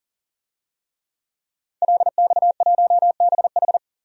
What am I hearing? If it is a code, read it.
Morse code, heard as a keyed tone, about 40 wpm, 700 Hz. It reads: LX1BH
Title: DXCC-100 Call Signs - 5 Characters - 40wpm